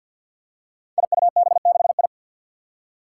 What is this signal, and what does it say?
Morse code, heard as a keyed tone, about 46 wpm, 700 Hz. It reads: IUB6I
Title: Five Challenging Letter-Number Combinations 50wpm